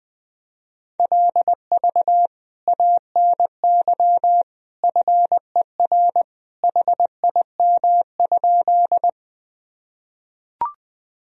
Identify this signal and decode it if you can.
Morse code, heard as a keyed tone, about 21 wpm, 700 Hz. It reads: LV ANY FER HIM?
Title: Sentences from Top 400 Words and Abbreviations 20wpm